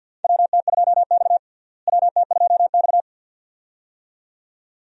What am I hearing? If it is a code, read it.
Morse code, heard as a keyed tone, about 51 wpm, 700 Hz. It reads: WT2X WT2X
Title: USA Call Signs - 4 Characters - Repeated 2x - 50wpm